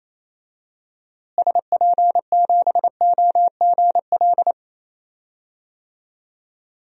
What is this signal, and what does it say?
Morse code, heard as a keyed tone, about 28 wpm, 700 Hz. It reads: SP7OGL
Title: DXCC-100 Call Signs - 6 Characters - Repeated 2x - 28wpm